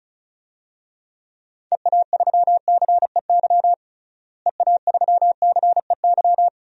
Morse code, heard as a keyed tone, about 36 wpm, 700 Hz. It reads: EA3CEY EA3CEY
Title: DXCC-100 Call Signs - 6 Characters - Repeated 2x - 35wpm